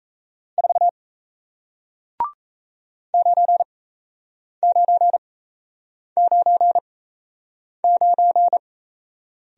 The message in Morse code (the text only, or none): 4 9999